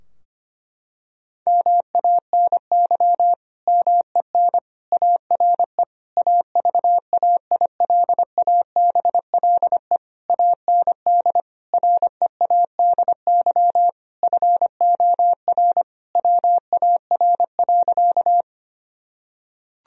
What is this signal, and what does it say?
Morse code, heard as a keyed tone, about 25 wpm, 700 Hz. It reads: MANY MEN ARE AVAILABLE AND READY FOR WAR.
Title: Sentences from Top 1000 Words - Review Part 3 - 25wpm